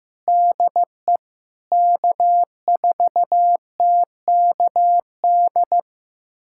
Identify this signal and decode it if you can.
Morse code, heard as a keyed tone, about 15 wpm, 700 Hz. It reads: DE K4TKD
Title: USA Callsign DE Callsign - 5 Characters - 15wpm